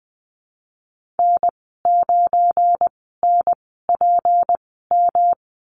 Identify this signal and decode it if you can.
Morse code, heard as a keyed tone, about 20 wpm, spaced as for 15 wpm, 700 Hz. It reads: N9NPM
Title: 186 Lesson 30g - Letter b - Call Signs 20wpm